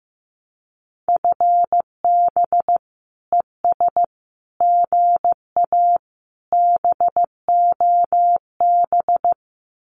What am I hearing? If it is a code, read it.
Morse code, heard as a keyed tone, about 15 wpm, 700 Hz. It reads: FB ES GA BOB
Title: Common QSO Phrases Encore 15wpm